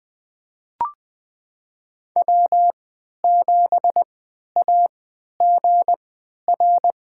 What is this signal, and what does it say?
Morse code, heard as a keyed tone, about 20 wpm, spaced as for 11 wpm, 700 Hz. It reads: W7AGR
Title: USA Call Signs - 5 Characters - 3x Character Spacing - 20wpm